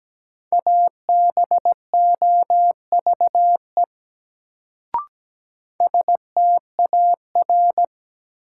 Morse code, heard as a keyed tone, about 17 wpm, 700 Hz. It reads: ABOVE STAR
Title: Sentences from Top 1000 Words - Review Part 2 - New Word Drill - 17wpm